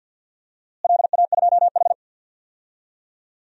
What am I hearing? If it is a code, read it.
Morse code, heard as a keyed tone, about 52 wpm, 700 Hz. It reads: LA2H
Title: DXCC-100 Call Signs - 4 Characters - Repeated 2x - 50wpm